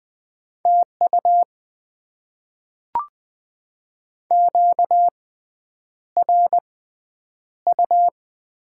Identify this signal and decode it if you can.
Morse code, heard as a keyed tone, about 20 wpm, spaced as for 9 wpm, 700 Hz. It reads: TU QRU
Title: Top 90 QSO Elements - 6x Character Spacing - 20wpm